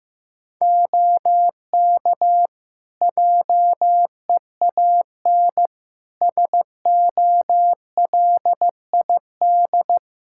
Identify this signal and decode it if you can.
Morse code, heard as a keyed tone, about 15 wpm, 700 Hz. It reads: OK JEAN SOLID
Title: Practice QSO - Exchange 3 - More Info - 15wpm